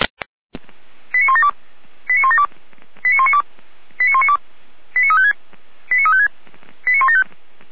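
Five-tone selective call (ZVEI), heard as quick runs of five tones, tones of about 70 ms. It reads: 89172 89172 89192 89192 89356 89356 89175
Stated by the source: ZVEI five-tone selective call received via SDR.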